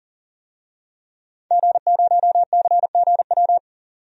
Morse code, heard as a keyed tone, about 40 wpm, 700 Hz. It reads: G0CGW
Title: DXCC-100 Call Signs - 5 Characters - 40wpm